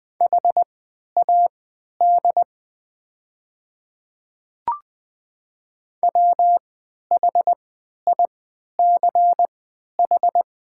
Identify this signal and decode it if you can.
Morse code, heard as a keyed tone, about 20 wpm, spaced as for 11 wpm, 700 Hz. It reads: HAD WHICH
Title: 3x Character Spacing - Top 100 Words - 20wpm